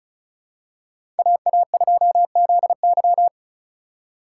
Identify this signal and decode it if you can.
Morse code, heard as a keyed tone, about 36 wpm, 700 Hz. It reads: AA2ZY
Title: USA Call Signs - 5 Characters - 4x Character Spacing - 35wpm